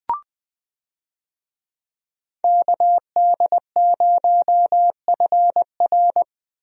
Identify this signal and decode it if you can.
Morse code, heard as a keyed tone, about 20 wpm, 700 Hz. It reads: KD0FR
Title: USA Call Signs - 5 Characters - Repeated 3x - 20wpm